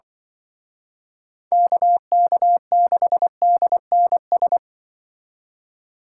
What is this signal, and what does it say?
Morse code, heard as a keyed tone, about 25 wpm, 700 Hz. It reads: KK6DNS
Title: Speed Racing - USA 2x3 Call Signs 20wpm